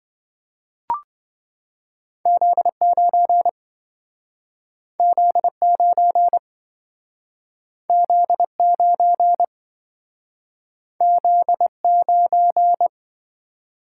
Morse code, heard as a keyed tone, about 26 wpm, 700 Hz. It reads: Z9 Z9 Z9 Z9